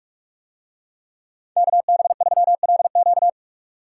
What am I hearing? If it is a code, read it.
Morse code, heard as a keyed tone, about 42 wpm, 700 Hz. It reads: KB3LQ